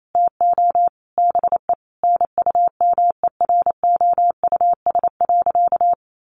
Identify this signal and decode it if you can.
Morse code, heard as a keyed tone, about 28 wpm, 700 Hz. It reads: TO BE NUMEROUS.
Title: Sentences from Top 1400 Words 28wpm